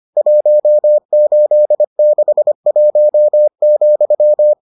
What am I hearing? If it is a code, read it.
Morse code, heard as a keyed tone, about 25 wpm, 600 Hz. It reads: 1861,